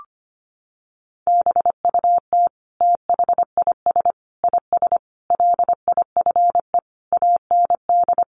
Morse code, heard as a keyed tone, about 25 wpm, 700 Hz. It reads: BUT THIS IS LIFE AND
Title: Sentences from Top 1100 Words 25wpm